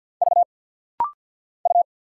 Morse code, heard as a keyed tone, about 50 wpm, 700 Hz. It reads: V U
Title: Single Letter-Number Alt - Rapid-Fire 50wpm